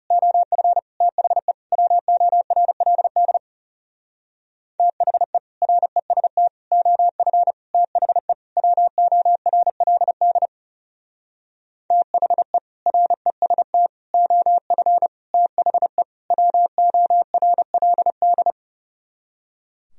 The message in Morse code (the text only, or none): OF THE WORLD THE REST OF THE WORLD THE REST OF THE WORLD